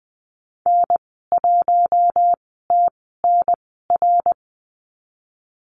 Morse code, heard as a keyed tone, about 19 wpm, spaced as for 14 wpm, 700 Hz. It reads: N1TNR
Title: 062 Lesson 11g - Letter h - Call Signs 20wpm